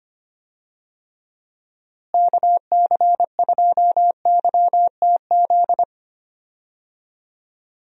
Morse code, heard as a keyed tone, about 25 wpm, 700 Hz. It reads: KC2YTZ
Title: USA Call Signs - 6 Characters - Repeated 2x - 25wpm